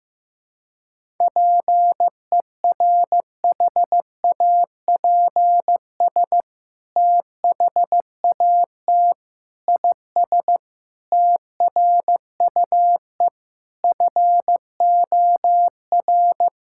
Morse code, heard as a keyed tone, about 15 wpm, 700 Hz. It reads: PERHAPS THAT IS TRUE FOR
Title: Sentences from Top 1000 Words - Review Part 5 - 15wpm